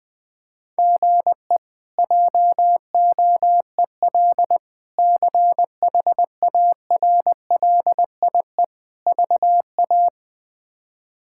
Morse code, heard as a keyed tone, about 21 wpm, 700 Hz. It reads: GE JOEL CHARLIE VA
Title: K1USN SST QSO Practice - 20wpm